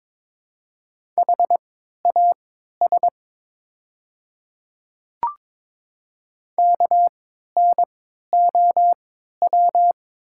Morse code, heard as a keyed tone, about 23 wpm, spaced as for 13 wpm, 700 Hz. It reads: HAS KNOW
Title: 3x Character Spacing - Top 100 Words - 22wpm